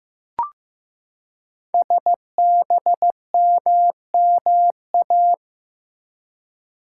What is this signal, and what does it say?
Morse code, heard as a keyed tone, about 15 wpm, 700 Hz. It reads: SBMMA
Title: Code Groups - 5 Letters - 15wpm